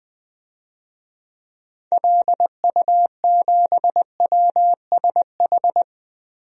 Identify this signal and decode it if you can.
Morse code, heard as a keyed tone, about 20 wpm, 700 Hz. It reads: LU7WSH
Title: DXCC-100 Call Signs - 6 Characters - Repeated 3x - 20wpm